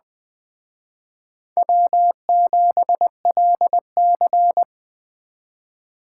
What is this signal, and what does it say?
Morse code, heard as a keyed tone, about 21 wpm, 700 Hz. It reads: W7LC
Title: Speed Racing - USA 1x2 Call Signs 17wpm